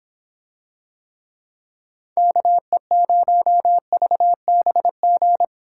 Morse code, heard as a keyed tone, about 25 wpm, 700 Hz. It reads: KE0VBG